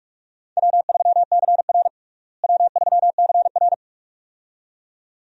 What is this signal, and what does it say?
Morse code, heard as a keyed tone, about 43 wpm, 700 Hz. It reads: W3CR W3CR
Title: USA Call Signs - 4 Characters - Repeated 2x - 45wpm